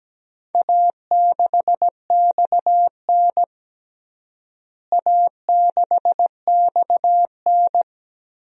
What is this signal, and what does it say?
Morse code, heard as a keyed tone, about 17 wpm, 700 Hz. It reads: A6XN A6XN
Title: DXCC-100 Call Signs - 4 Characters - Repeated 3x - 17wpm